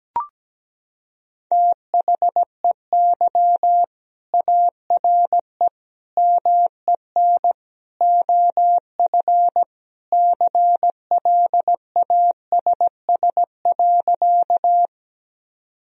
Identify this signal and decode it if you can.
Morse code, heard as a keyed tone, about 17 wpm, 700 Hz. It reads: THEY ARE MEN OF CLASS.
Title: Sentences from Top 1000 Words - Review Part 2 - 17wpm